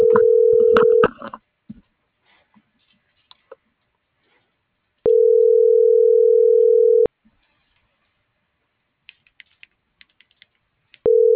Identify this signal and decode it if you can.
Ringback tone (North America), 440 + 480 Hz.